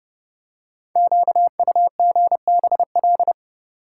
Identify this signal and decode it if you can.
Morse code, heard as a keyed tone, about 29 wpm, 700 Hz. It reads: QUGBL
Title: Code Groups - 5 Letters - 30wpm